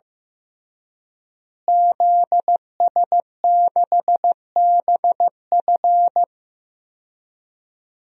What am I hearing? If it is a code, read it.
Morse code, heard as a keyed tone, about 16 wpm, 700 Hz. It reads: ZS6BF
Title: DXCC-100 Call Signs - 5 Characters - Repeated 2x - 15wpm